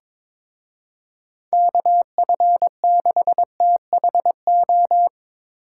Morse code, heard as a keyed tone, about 25 wpm, 700 Hz. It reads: KF6THO
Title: Speed Racing - USA Call Signs 22wpm